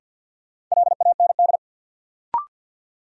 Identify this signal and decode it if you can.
Morse code, heard as a keyed tone, about 51 wpm, 700 Hz. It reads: LAND
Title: Sentences from Top 500 Words - New Word Drill - 50wpm